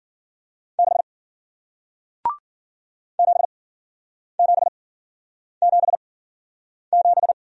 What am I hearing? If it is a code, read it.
Morse code, heard as a keyed tone, about 53 wpm, spaced as for 9 wpm, 700 Hz. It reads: 6 7777